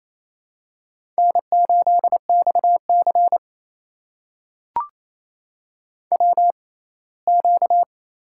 Morse code, heard as a keyed tone, about 29 wpm, 700 Hz. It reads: N 8 X C W Q
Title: USA Call Signs - 4 Characters - 6x Character Spacing - 28wpm